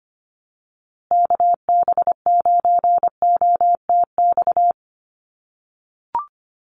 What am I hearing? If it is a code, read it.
Morse code, heard as a keyed tone, about 25 wpm, 700 Hz. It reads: KB9OTX
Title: USA Call Signs Repeated 2x 25wpm